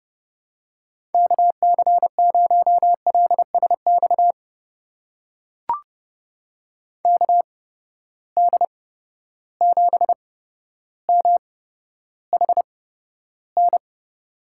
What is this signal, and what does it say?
Morse code, heard as a keyed tone, about 31 wpm, spaced as for 13 wpm, 700 Hz. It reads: KC0LSX KD7MHN
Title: USA Call Signs - 6 Characters - 8x Character Spacing - 30wpm